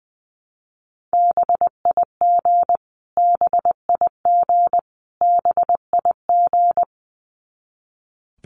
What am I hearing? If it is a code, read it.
Morse code, heard as a keyed tone, about 20 wpm, 700 Hz. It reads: BIG BIG BIG
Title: Top 500 Words Repeated 20wpm